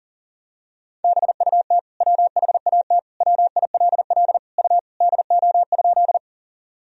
Morse code, heard as a keyed tone, about 37 wpm, 700 Hz. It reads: BUT WHAT WILL U DO?
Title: Sentences from Top 100 Words and Abbreviations 40wpm